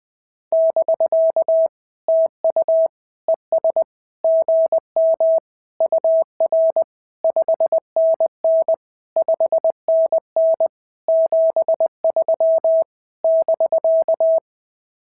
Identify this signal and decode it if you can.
Morse code, heard as a keyed tone, about 18 wpm, 650 Hz. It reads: <BK> TU ES GM UR 5NN 5NN 73 <BK>
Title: SOTA QSO Practice 20wpm